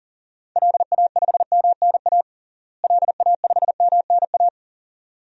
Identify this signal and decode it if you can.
Morse code, heard as a keyed tone, about 40 wpm, 700 Hz. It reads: LA5MNA LA5MNA
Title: DXCC-100 Call Signs - 6 Characters - Repeated 2x - 40wpm